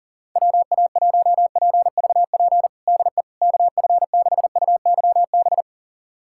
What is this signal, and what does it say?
Morse code, heard as a keyed tone, about 40 wpm, 700 Hz. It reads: WA1PVP DE KF6UYB
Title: USA Callsign DE Callsign - 6 Characters - 40wpm